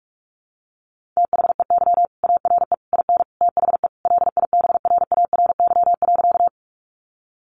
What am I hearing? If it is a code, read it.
Morse code, heard as a keyed tone, about 44 wpm, 700 Hz. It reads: THEY ARE IN THE LIBRARY.
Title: Sentences from Top 600 Words 45wpm